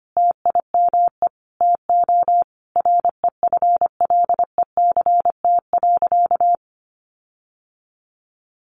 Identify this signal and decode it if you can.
Morse code, heard as a keyed tone, about 25 wpm, 700 Hz. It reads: TIME TO REFLECT.
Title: Sentences from Top 1500 Words 25wpm